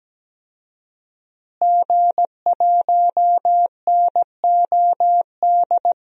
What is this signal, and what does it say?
Morse code, heard as a keyed tone, about 17 wpm, 700 Hz. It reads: G1NOD
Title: DXCC-100 Call Signs - 5 Characters - Repeated 2x - 17wpm